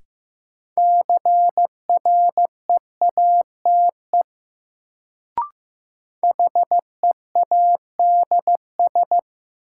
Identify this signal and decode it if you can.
Morse code, heard as a keyed tone, about 15 wpm, 700 Hz. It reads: CREATE HEADS
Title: Sentences from Top 1000 Words - Review Part 3 - New Word Drill - 15wpm